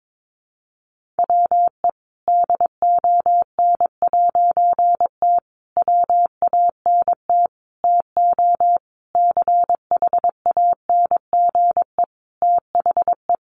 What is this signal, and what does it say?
Morse code, heard as a keyed tone, about 22 wpm, 700 Hz. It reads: WE DON'T WANT TO CHANGE THE
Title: A Sound Of Thunder 22wpm